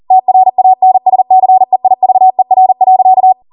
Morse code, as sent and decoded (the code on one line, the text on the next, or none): - .-. .- -. ... -.-. .. ...- . .-. .-.-.-
TRANSCEIVER.